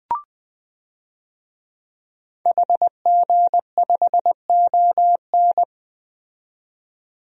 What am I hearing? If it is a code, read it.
Morse code, heard as a keyed tone, about 20 wpm, 700 Hz. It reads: HG5ON
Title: DXCC-100 Call Signs - 5 Characters - 20wpm